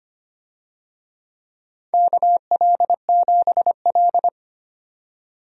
Morse code, heard as a keyed tone, about 25 wpm, 700 Hz. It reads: KL7L
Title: DXCC-100 Call Signs - 4 Characters - 25wpm